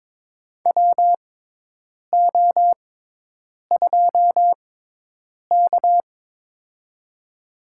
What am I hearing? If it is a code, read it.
Morse code, heard as a keyed tone, about 22 wpm, spaced as for 8 wpm, 700 Hz. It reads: WO2K
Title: USA Call Signs - 4 Characters - 6x Character Spacing - 22wpm